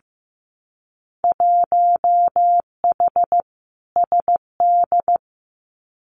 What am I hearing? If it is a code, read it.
Morse code, heard as a keyed tone, about 15 wpm, 700 Hz. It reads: 1H SD
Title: Winter Field Day Exchanges 15wpm